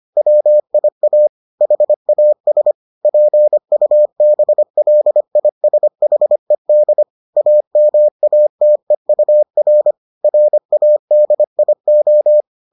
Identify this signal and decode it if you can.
Morse code, heard as a keyed tone, about 25 wpm, 600 Hz. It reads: WIA HAS PUBLISHED AMATEUR RADIO